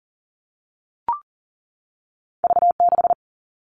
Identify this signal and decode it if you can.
Morse code, heard as a keyed tone, about 41 wpm, 700 Hz. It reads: V6
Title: All Cut Numbers - 2-Digit 40wpm